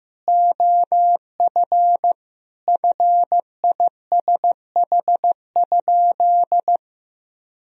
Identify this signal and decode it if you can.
Morse code, heard as a keyed tone, about 15 wpm, 700 Hz. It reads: OF FISH?
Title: Sentences from Top 1000 Words - Review Part 1 - 15wpm